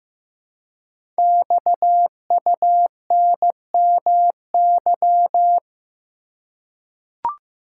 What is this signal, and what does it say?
Morse code, heard as a keyed tone, about 15 wpm, 700 Hz. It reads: XUNMY
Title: Code Groups - 5 Letters - 15wpm